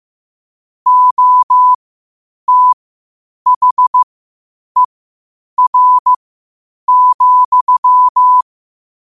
Morse code, heard as a keyed tone, about 15 wpm, spaced as for 8 wpm, 1000 Hz. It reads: OTHER,
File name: qotdfiles / 8wpm